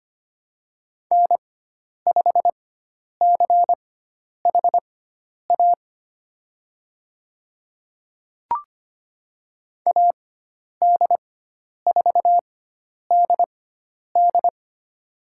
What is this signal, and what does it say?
Morse code, heard as a keyed tone, about 25 wpm, spaced as for 10 wpm, 700 Hz. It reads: N5CHA AD4DD
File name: Lesson 16g - Letter c - Call Signs 25wpm